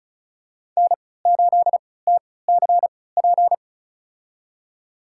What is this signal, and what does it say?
Morse code, heard as a keyed tone, about 36 wpm, spaced as for 20 wpm, 690 Hz. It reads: N8TCP
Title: USA Call Signs - 5 Characters - 3x Character Spacing - 35wpm